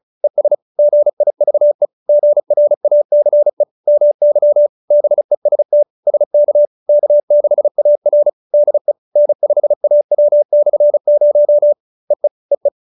Morse code, heard as a keyed tone, about 35 wpm, 590 Hz. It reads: ES GIVE GRACE MY BEST SK K6AR DE N5AW/0 EE EE
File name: QSO312 35